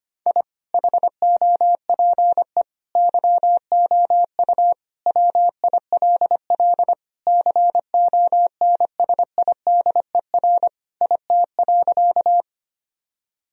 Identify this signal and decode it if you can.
Morse code, heard as a keyed tone, about 25 wpm, 700 Hz. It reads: I HOPE YOU WILL CONSIDER IT.
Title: Sentences from Top 1000 Words - Review Part 4 - 25wpm